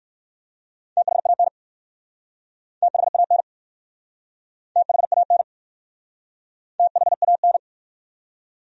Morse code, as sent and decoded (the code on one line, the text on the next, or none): - .... .- -. / - .... .- -. / - .... .- -. / - .... .- -.
THAN THAN THAN THAN